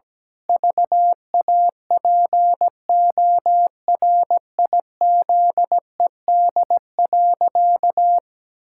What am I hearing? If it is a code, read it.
Morse code, heard as a keyed tone, about 17 wpm, 700 Hz. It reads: VAPORIZED.
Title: Pangram Challenge v2 - 17wpm